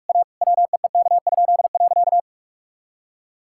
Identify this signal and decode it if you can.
Morse code, heard as a keyed tone, about 46 wpm, 700 Hz. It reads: A WEEK?.
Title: Pangram Challenge v2 - 45wpm